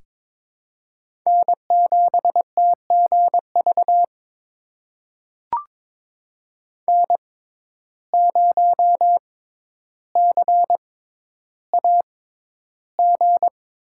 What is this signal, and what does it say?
Morse code, heard as a keyed tone, about 21 wpm, spaced as for 11 wpm, 700 Hz. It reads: N7TGV N0CAG
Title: USA Call Signs - 5 Characters - 6x Character Spacing - 22wpm